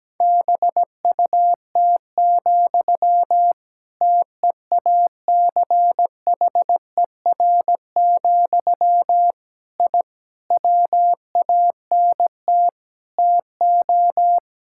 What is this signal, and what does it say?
Morse code, heard as a keyed tone, about 17 wpm, 700 Hz. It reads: BUT, TEACHER, I WANT TO
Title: Sentences from Top 1000 Words - Review Part 6 - 17wpm